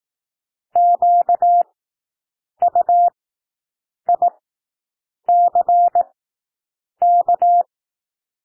Morse code, heard as a keyed tone, about 18 wpm, spaced as for 7 wpm, 700 Hz. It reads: QUICK
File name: SOUND WORDS 18 wpm